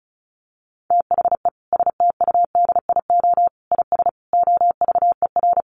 Morse code, heard as a keyed tone, about 34 wpm, 700 Hz. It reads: THE STUDIO IS OVER
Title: Sentences from Top 1200 Words 35wpm